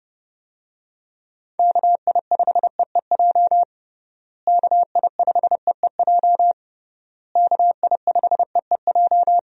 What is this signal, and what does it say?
Morse code, heard as a keyed tone, about 30 wpm, 700 Hz. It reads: KI5EEJ KI5EEJ KI5EEJ
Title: USA Call Signs - 6 Characters - Repeated 3x - 30wpm